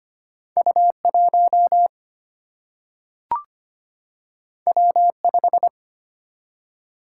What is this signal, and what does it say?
Morse code, heard as a keyed tone, about 25 wpm, 700 Hz. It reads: U 1 W 5
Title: Single-Letter Single-Number 25wpm